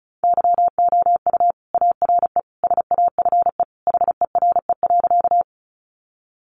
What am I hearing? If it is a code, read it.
Morse code, heard as a keyed tone, about 34 wpm, 700 Hz. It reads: YOU ARE SAFE HERE.
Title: Sentences from Top 900 Words 35wpm